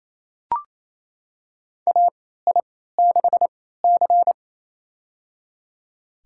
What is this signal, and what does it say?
Morse code, heard as a keyed tone, about 28 wpm, spaced as for 16 wpm, 700 Hz. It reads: AI6C
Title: USA Call Signs - 3x Character Spacing - 28wpm